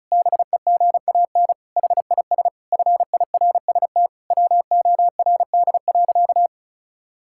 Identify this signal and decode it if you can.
Morse code, heard as a keyed tone, about 36 wpm, 700 Hz. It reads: BEGAN HIS FIRST WORD.
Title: Pangram Challenge v2 - 35wpm